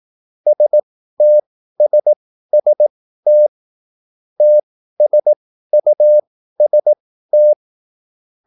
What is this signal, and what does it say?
Morse code, heard as a keyed tone, about 18 wpm, spaced as for 13 wpm, 600 Hz. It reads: STSST TSUST